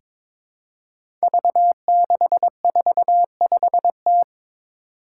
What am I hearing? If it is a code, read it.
Morse code, heard as a keyed tone, about 22 wpm, 700 Hz. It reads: V645T
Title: Five Challenging Letter-Number Combinations 22wpm